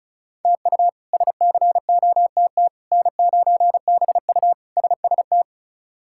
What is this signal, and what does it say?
Morse code, heard as a keyed tone, about 34 wpm, 700 Hz. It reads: TU SCOTT N9BU SST
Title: K1USN SST QSO Practice - 35wpm